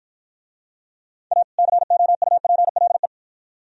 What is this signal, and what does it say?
Morse code, heard as a keyed tone, about 53 wpm, 700 Hz. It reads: A COUPLE